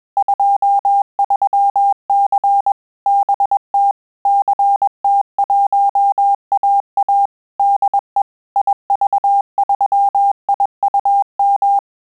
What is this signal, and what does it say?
Morse code, heard as a keyed tone, about 21 wpm, 800 Hz. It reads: 23C BT CT1AA DE IV3IUM
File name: QSO CW 100 CPM